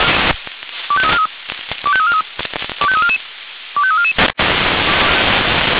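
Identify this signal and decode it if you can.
Five-tone selective call (ZVEI), heard as quick runs of five tones, tones of about 70 ms. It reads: 25343 25343 25344 25344
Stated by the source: Received by SDR; ZVEI five-tone call 25212, channel 496.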